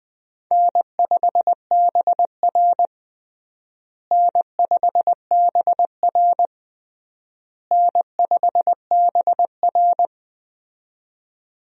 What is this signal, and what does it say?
Morse code, heard as a keyed tone, about 20 wpm, 700 Hz. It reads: N5BR N5BR N5BR
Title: USA Call Signs - 4 Characters - Repeated 3x - 20wpm